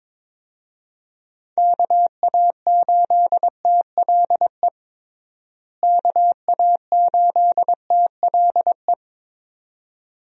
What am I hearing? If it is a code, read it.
Morse code, heard as a keyed tone, about 23 wpm, 700 Hz. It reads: KA8TLE KA8TLE
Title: USA Call Signs - 6 Characters - Repeated 2x - 22wpm